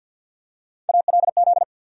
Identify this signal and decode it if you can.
Morse code, heard as a keyed tone, about 48 wpm, 700 Hz. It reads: ALZ
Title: USA Call Sign Postfixes - 3 Letters 50wpm